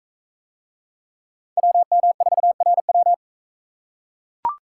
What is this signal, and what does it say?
Morse code, heard as a keyed tone, about 42 wpm, 700 Hz. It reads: WM4RW